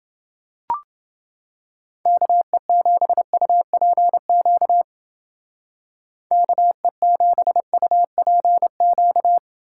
Morse code, heard as a keyed tone, about 31 wpm, 700 Hz. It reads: KE7UPQ KE7UPQ